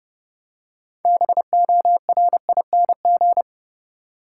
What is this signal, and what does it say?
Morse code, heard as a keyed tone, about 30 wpm, 700 Hz. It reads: BORING
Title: Six Letter Words - 30wpm